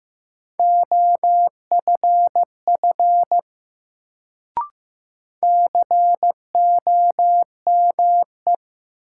Morse code, heard as a keyed tone, about 15 wpm, 700 Hz. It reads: OFF COME
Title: Sets of 2 Words - New Word Drill - 15wpm